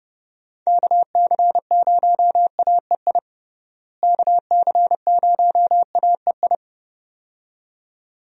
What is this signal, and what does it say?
Morse code, heard as a keyed tone, about 32 wpm, 700 Hz. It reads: KC0AEI KC0AEI
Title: USA Call Signs - 6 Characters - Repeated 2x - 30wpm